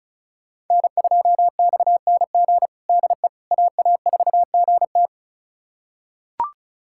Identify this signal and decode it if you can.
Morse code, heard as a keyed tone, about 34 wpm, 690 Hz. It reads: N2XNG DE AA4GT
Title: USA Callsign DE Callsign - 5 Characters - 35wpm